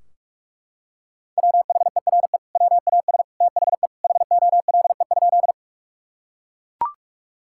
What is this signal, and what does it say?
Morse code, heard as a keyed tone, about 42 wpm, 700 Hz. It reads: WHERE WAS THE HOLE?
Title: Sentences from Top 1600 Words 45wpm